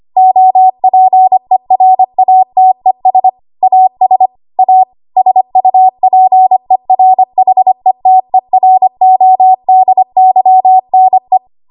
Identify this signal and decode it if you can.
Morse code, heard as a keyed tone, about 25 wpm, 750 Hz. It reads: OPERATES AS A SUPERHETERODYNE
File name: Archive / 25 WPM / 25WPM